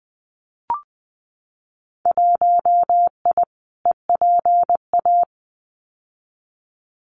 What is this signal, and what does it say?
Morse code, heard as a keyed tone, about 20 wpm, 700 Hz. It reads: 1I EPA
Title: Winter Field Day Exchanges Encore 20wpm